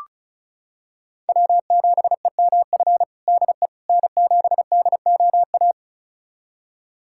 Morse code, heard as a keyed tone, about 36 wpm, 700 Hz. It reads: W7EMF DE N7DOA
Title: USA Callsign DE Callsign - 5 Characters - 35wpm